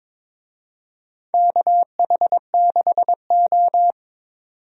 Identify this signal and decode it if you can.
Morse code, heard as a keyed tone, about 23 wpm, 690 Hz. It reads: KH6O
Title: DXCC-100 Call Signs - 4 Characters - 22wpm